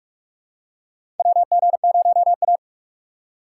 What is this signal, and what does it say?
Morse code, heard as a keyed tone, about 43 wpm, 700 Hz. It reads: WG0A
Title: USA Call Signs - 3x Character Spacing - 45wpm